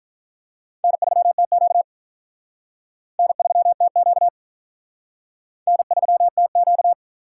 Morse code, heard as a keyed tone, about 43 wpm, 700 Hz. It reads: N3TQ N3TQ N3TQ
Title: Speed Racing - USA 1x2 Call Signs 35wpm